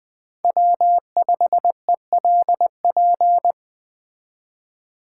Morse code, heard as a keyed tone, about 20 wpm, 700 Hz. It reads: W5ELP
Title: Speed Racing - USA 1x3 Call Signs 17wpm